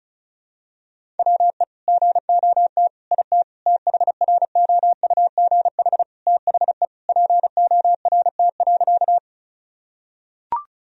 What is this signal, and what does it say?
Morse code, heard as a keyed tone, about 36 wpm, 700 Hz. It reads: WE GOT IT THROUGH THE PORT.
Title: Sentences from Top 1000 Words - Review Part 1 - 35wpm